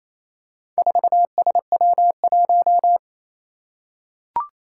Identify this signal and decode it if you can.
Morse code, heard as a keyed tone, about 28 wpm, 700 Hz. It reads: 4SW1
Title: Four Challenging Letter-Number Combinations 28wpm